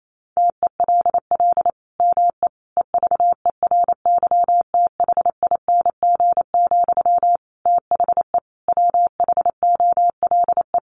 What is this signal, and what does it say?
Morse code, heard as a keyed tone, about 28 wpm, 700 Hz. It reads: TELL ME EVERYTHING, THE WHOLE
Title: Sentences from Top 800 Words 28wpm